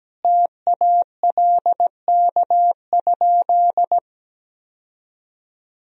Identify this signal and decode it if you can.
Morse code, heard as a keyed tone, about 17 wpm, 700 Hz. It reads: TALK?
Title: Sentences from Top 1000 Words - Review Part 1 - 17wpm